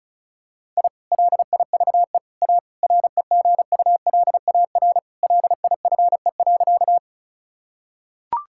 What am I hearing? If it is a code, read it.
Morse code, heard as a keyed tone, about 36 wpm, 700 Hz. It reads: I LIVE A REGULAR LIFE.
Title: Sentences from Top 1000 Words - Review Part 6 - 35wpm